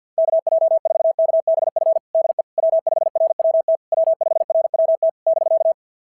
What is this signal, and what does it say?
Morse code, heard as a keyed tone, about 52 wpm, 690 Hz. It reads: KJ4KBF DE W5RWT W5RWT <BK>
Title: SOTA QSO Practice 50wpm